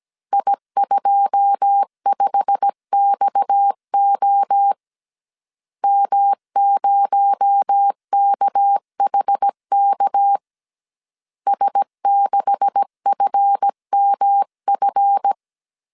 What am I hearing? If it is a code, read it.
Morse code, heard as a keyed tone, about 17 wpm, 800 Hz. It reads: I25XO M0KHK S6FMF
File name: CW / DET / 1 9 1500C S C 16wpm 2909